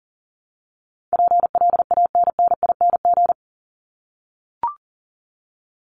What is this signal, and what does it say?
Morse code, heard as a keyed tone, about 40 wpm, 700 Hz. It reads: PLANNING
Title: ING Suffix - Top 100 Words with Base Words 40wpm